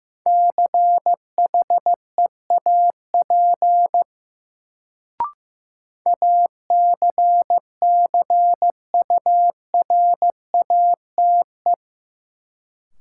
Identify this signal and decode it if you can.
Morse code, heard as a keyed tone, about 16 wpm, 700 Hz. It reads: CHEAP ACCURATE
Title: Sentences from Top 1200 Words - New Word Drill Alt - 15wpm